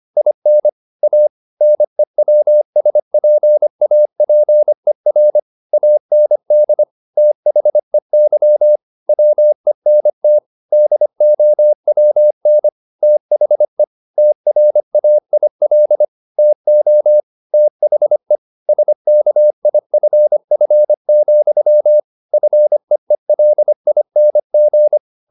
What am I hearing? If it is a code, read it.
Morse code, heard as a keyed tone, about 25 wpm, 600 Hz. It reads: IN A NEWSPAPER AND THEY WENT DOWN THE TRAIL TO THE SKIFF, FEELING
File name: the old man and the sea part 1 25wpm